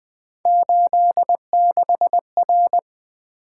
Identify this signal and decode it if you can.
Morse code, heard as a keyed tone, about 20 wpm, 700 Hz. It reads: 86R
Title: Three Letter-Number Combinations - 20wpm